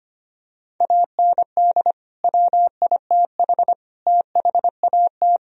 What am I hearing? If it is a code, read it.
Morse code, heard as a keyed tone, about 25 wpm, 700 Hz. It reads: AND WITH THAT
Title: Sentences from Top 1000 Words - Review Part 1 - 25wpm